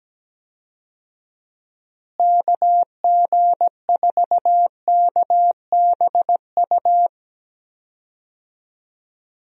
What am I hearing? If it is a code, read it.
Morse code, heard as a keyed tone, about 17 wpm, 700 Hz. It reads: KG4KBU
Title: USA Call Signs - 6 Characters - Repeated 2x - 17wpm